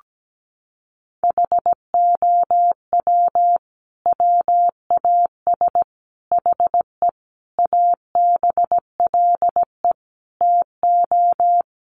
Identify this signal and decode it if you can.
Morse code, heard as a keyed tone, about 17 wpm, 700 Hz. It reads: HOW WAS HE ABLE TO
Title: Sentences from Top 1400 Words 17wpm